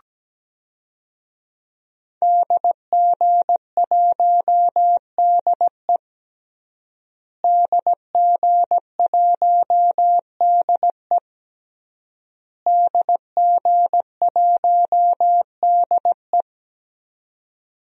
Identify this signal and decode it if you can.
Morse code, heard as a keyed tone, about 17 wpm, 700 Hz. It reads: DG1DE DG1DE DG1DE
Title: DXCC-100 Call Signs - 5 Characters - Repeated 3x - 17wpm